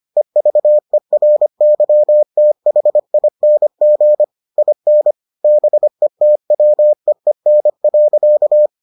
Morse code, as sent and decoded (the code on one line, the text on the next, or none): . ...- . .-. -.-- - .... .. -. --. / .. -. / -... - .-- . -. .-.-.-
EVERYTHING IN BETWEEN.